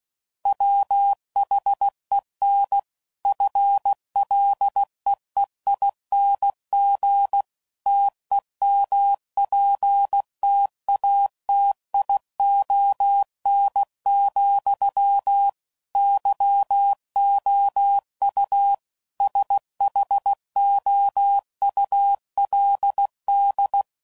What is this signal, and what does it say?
Morse code, heard as a keyed tone, about 16 wpm, 800 Hz. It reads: WHEN FLEEING TEMPTATION, YOU SHOULD